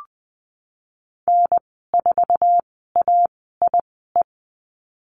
Morse code, heard as a keyed tone, about 20 wpm, spaced as for 15 wpm, 700 Hz. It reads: N4AIE
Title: 055 Lesson 10g - Letter r - Call Signs 20wpm